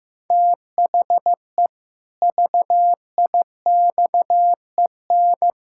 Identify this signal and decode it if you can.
Morse code, heard as a keyed tone, about 16 wpm, 700 Hz. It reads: THE VIXEN
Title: Pangram Challenge v2 - 15wpm